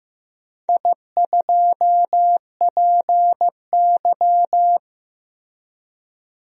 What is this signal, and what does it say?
Morse code, heard as a keyed tone, about 15 wpm, 700 Hz. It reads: I2PY
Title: DXCC-100 Call Signs - 4 Characters - Repeated 3x - 15wpm